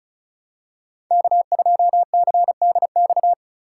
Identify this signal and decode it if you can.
Morse code, heard as a keyed tone, about 36 wpm, 700 Hz. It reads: K2CDX
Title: K1USN SST QSO Practice - 35wpm